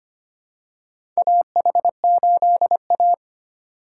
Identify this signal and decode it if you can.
Morse code, heard as a keyed tone, about 25 wpm, 700 Hz. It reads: AH8A
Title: USA Call Signs - 4 Characters - 2x Character Spacing - 25wpm